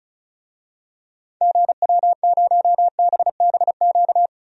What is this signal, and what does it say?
Morse code, heard as a keyed tone, about 36 wpm, 700 Hz. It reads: GW0BBQ
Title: DXCC-100 Call Signs - 6 Characters - 35wpm